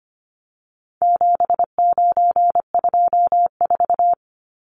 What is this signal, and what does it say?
Morse code, heard as a keyed tone, about 25 wpm, 700 Hz. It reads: 7924
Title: Numbers - 4-Digit 25wpm